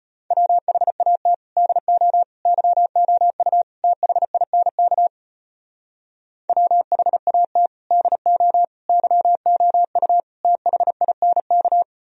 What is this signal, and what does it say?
Morse code, heard as a keyed tone, about 33 wpm, 700 Hz. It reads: WHAT DO YOU THINK WHAT DO YOU THINK